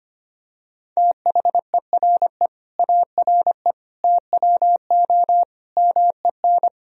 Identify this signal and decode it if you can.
Morse code, heard as a keyed tone, about 25 wpm, 700 Hz. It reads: THERE ARE TWO MEN
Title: Sentences from Top 1000 Words - Review Part 3 - 25wpm